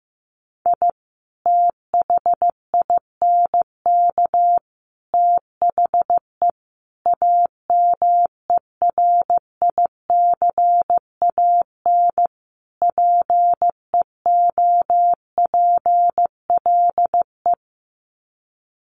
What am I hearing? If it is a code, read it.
Morse code, heard as a keyed tone, about 15 wpm, 700 Hz. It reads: I THINK THE AMERICAN PEOPLE
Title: Sets of 5 Words Encore 15wpm